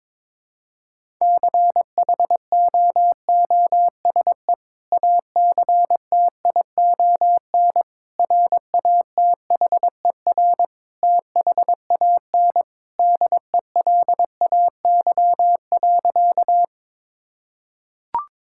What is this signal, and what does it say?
Morse code, heard as a keyed tone, about 22 wpm, 700 Hz. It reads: CHOOSE ACTION RATHER THAN DELAY.
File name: practice / 22wpm / Sentences with 2-Word Phrases - 22wpm